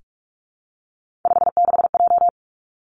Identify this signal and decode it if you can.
Morse code, heard as a keyed tone, about 46 wpm, 700 Hz. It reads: 56J
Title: Three Challenging Letter-Number Combinations 45wpm